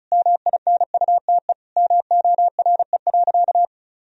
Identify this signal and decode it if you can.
Morse code, heard as a keyed tone, about 36 wpm, 700 Hz. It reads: MINUTE MORE.
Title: Sentences from Top 1000 Words - Review Part 4 - 35wpm